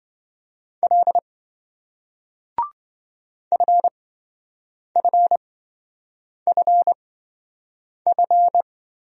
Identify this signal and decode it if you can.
Morse code, heard as a keyed tone, about 28 wpm, spaced as for 7 wpm, 700 Hz. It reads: L FFFF